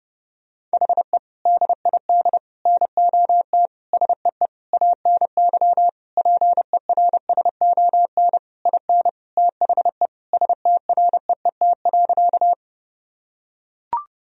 Morse code, heard as a keyed tone, about 28 wpm, 700 Hz. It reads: HE DID NOT SEE ANY PERSON IN THE STREET.
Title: Sentences from Top 1000 Words - Review Part 3 - 30wpm